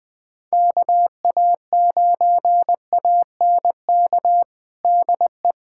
Morse code, heard as a keyed tone, about 20 wpm, 700 Hz. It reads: KA9ANK DE
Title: USA Callsign DE Callsign - 6 Characters - 20wpm